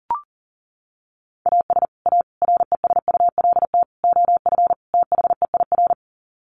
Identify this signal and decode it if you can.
Morse code, heard as a keyed tone, about 37 wpm, 700 Hz. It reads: AS A RESULT OF THEIR
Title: Sets of 5 Words Encore2 40wpm